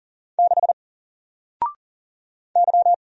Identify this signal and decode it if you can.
Morse code, heard as a keyed tone, about 40 wpm, 700 Hz. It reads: 6 Y
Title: Single Character (Letter:Number:Punctuation:Prosign) 40wpm